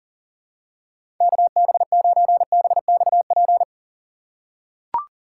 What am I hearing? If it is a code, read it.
Morse code, heard as a keyed tone, about 40 wpm, 700 Hz. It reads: KB9BXP
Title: USA Call Signs - 6 Characters - 2x Character Spacing - 40wpm